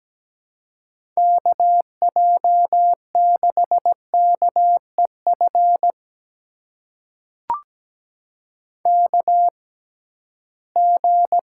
Morse code, heard as a keyed tone, about 17 wpm, spaced as for 13 wpm, 700 Hz. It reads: KJ6KEF KG
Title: USA Call Signs - 6 Characters - 6x Character Spacing - 17wpm